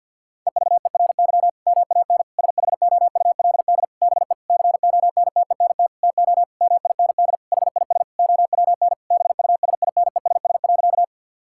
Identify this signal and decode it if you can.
Morse code, heard as a keyed tone, about 50 wpm, 700 Hz. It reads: EVERY MAN SHOULD BE CONTENT TO MIND HIS OWN BUSINESS.
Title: Sentences from Top 1000 Words - Review Part 6 - 50wpm